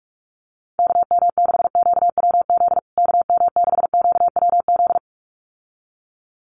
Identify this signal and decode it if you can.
Morse code, heard as a keyed tone, about 44 wpm, 700 Hz. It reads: KM6QWZ KM6QWZ
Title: USA Call Signs Repeated 2x 45wpm